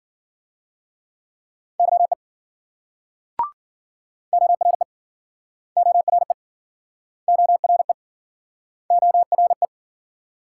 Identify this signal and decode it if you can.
Morse code, heard as a keyed tone, about 53 wpm, 700 Hz. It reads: YE ORE ORE ORE ORE